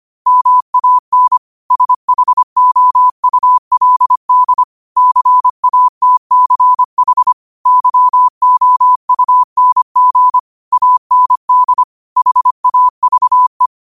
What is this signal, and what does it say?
Morse code, heard as a keyed tone, about 25 wpm, 1000 Hz. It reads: MAN SHOULD CATCH YOUNG AND HAVE
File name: qotdfiles / 25wpm